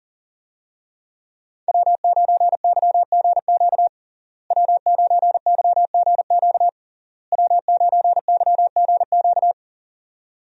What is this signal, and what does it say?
Morse code, heard as a keyed tone, about 40 wpm, 700 Hz. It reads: W9YGQ W9YGQ W9YGQ
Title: USA Call Signs - 5 Characters - Repeated 3x - 40wpm